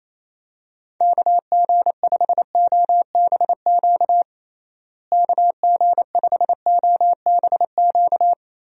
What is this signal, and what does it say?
Morse code, heard as a keyed tone, about 28 wpm, 700 Hz. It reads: KG5OBQ KG5OBQ
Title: USA Call Signs - 6 Characters - Repeated 2x - 28wpm